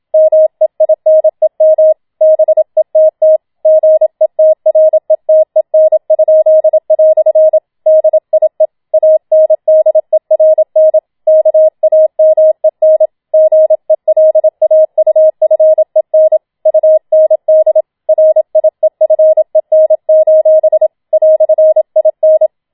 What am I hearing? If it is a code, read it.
Morse code, heard as a keyed tone, about 27 wpm, 630 Hz. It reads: MEINEM BETT GETRETEN?" DIE ANDERN KAMEN GELAUFEN UND RIEFEN: "IN
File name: Schneewittchen Teil2 27wpm